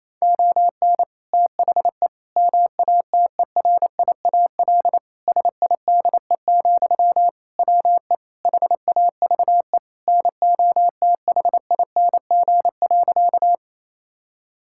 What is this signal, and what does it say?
Morse code, heard as a keyed tone, about 27 wpm, 700 Hz. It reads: ON THE MATERIAL SIDE, WE HAVE NOTHING.
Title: Sentences from Top 1000 Words - Review Part 5 - 28wpm